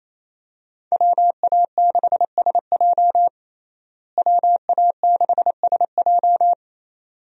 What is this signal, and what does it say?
Morse code, heard as a keyed tone, about 28 wpm, 700 Hz. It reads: WA6SJ WA6SJ
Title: USA Call Signs - 5 Characters - Repeated 2x - 28wpm